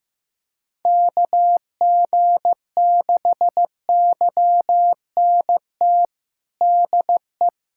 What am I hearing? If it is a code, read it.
Morse code, heard as a keyed tone, about 15 wpm, 700 Hz. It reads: KG6YNT DE
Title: USA Callsign DE Callsign - 6 Characters - 15wpm